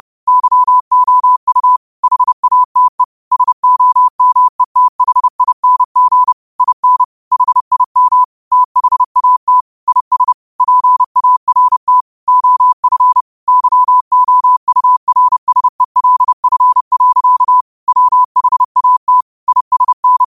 Morse code, heard as a keyed tone, about 30 wpm, 1000 Hz. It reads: YOU HATE SOMETHING IN HIM THAT IS PART OF YOURSELF. WHAT ISN